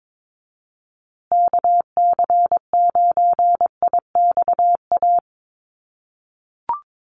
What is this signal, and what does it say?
Morse code, heard as a keyed tone, about 22 wpm, 700 Hz. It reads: KC9IXA
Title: USA Call Signs Repeated 3x 22wpm